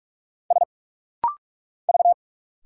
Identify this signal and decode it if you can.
Morse code, heard as a keyed tone, about 45 wpm, 690 Hz. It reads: S V
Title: Single Letters - Rapid-Fire 45wpm